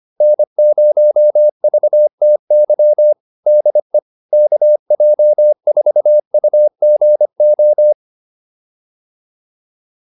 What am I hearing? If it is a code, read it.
Morse code, heard as a keyed tone, about 25 wpm, 600 Hz. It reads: N0VTY DE KJ4UGO
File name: QSO201 25